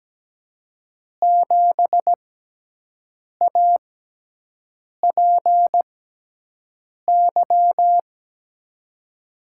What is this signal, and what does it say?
Morse code, heard as a keyed tone, about 17 wpm, spaced as for 6 wpm, 700 Hz. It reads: 7APY
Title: USA Call Signs - 6x Character Spacing - 17wpm